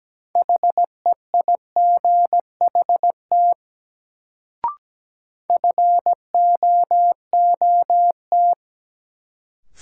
Morse code, heard as a keyed tone, about 17 wpm, 700 Hz. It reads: HEIGHT FOOT
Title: Sets of 5 Words - New Word Drill - 17wpm